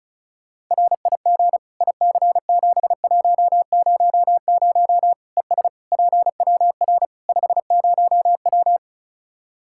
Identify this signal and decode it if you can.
Morse code, heard as a keyed tone, about 36 wpm, 700 Hz. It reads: RIG IC7100 ES PWR 50W
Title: Practice QSO - Exchange 2 - Info - 35wpm